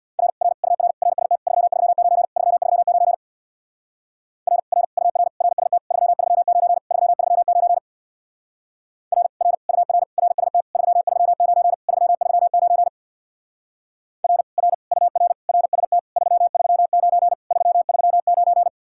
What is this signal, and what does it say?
Morse code, heard as a keyed tone, about 66 wpm, 700 Hz. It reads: R R UR RST 339 339 R R UR RST 339 339 R R UR RST 339 339 R R UR RST 339 339